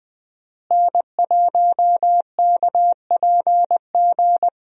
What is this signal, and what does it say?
Morse code, heard as a keyed tone, about 20 wpm, 700 Hz. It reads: N1KPG
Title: USA Call Signs - 5 Characters - Repeated 2x - 20wpm